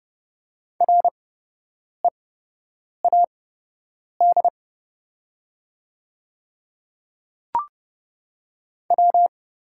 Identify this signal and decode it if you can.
Morse code, heard as a keyed tone, about 30 wpm, spaced as for 8 wpm, 700 Hz. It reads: READ W
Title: 8x Character Spacing - Top 300 Words - 30wpm